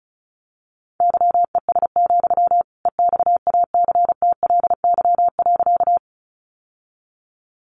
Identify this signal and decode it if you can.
Morse code, heard as a keyed tone, about 34 wpm, 700 Hz. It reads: YES, EXACTLY.
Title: Sentences from Top 900 Words 35wpm